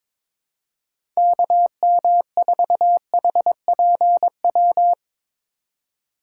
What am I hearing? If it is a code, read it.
Morse code, heard as a keyed tone, about 22 wpm, 700 Hz. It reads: KM4HPW
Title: USA Call Signs - 6 Characters - 4x Character Spacing - 22wpm